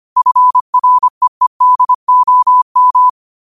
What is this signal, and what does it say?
Morse code, heard as a keyed tone, about 25 wpm, 1000 Hz. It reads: FREEDOM
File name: qotdfiles / 25wpm